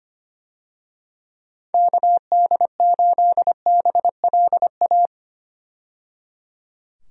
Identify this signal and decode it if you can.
Morse code, heard as a keyed tone, about 25 wpm, 700 Hz. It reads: KD8BLA
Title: USA Call Signs - 6 Characters - 25wpm